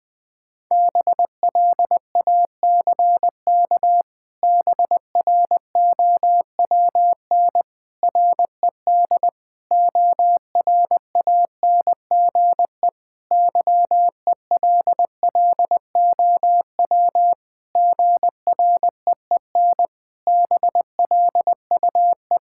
Morse code, heard as a keyed tone, about 20 wpm, 700 Hz. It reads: BLACK BROWN RED ORANGE YELLOW GREEN BLUE
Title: Pangram Challenge v2 - 20wpm